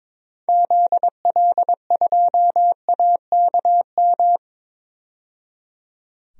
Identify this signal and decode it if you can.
Morse code, heard as a keyed tone, about 22 wpm, 700 Hz. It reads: ZL2AKM
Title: DXCC-100 Call Signs - 6 Characters - 22wpm